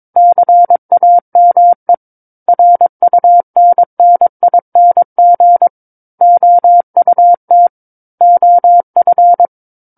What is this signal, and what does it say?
Morse code, heard as a keyed tone, about 22 wpm, 700 Hz. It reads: CAME RUNNING OUT OF
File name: 06-Chapter-06 22 WPM